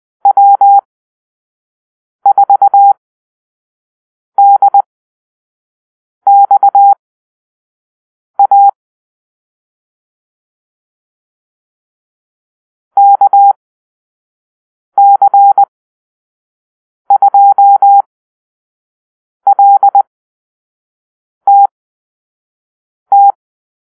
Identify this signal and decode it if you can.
Morse code, heard as a keyed tone, about 20 wpm, spaced as for 5 wpm, 800 Hz. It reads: W4DXA KC2LTT
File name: Call-Sign Drill